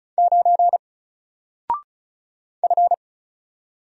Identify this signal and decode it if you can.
Morse code, heard as a keyed tone, about 36 wpm, 700 Hz. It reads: 9 F
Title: Single Character (Letter:Number:Punctuation:Prosign) Alt 35wpm